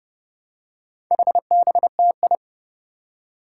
Morse code, heard as a keyed tone, about 28 wpm, 700 Hz. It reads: HBTI
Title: Four Challenging Letter-Number Combinations 30wpm